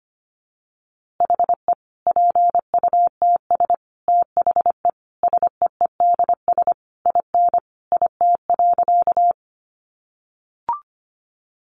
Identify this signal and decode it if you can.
Morse code, heard as a keyed tone, about 25 wpm, 700 Hz. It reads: HE PUTS THE SEEDS IN IT.
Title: Easy Phrases 25wpm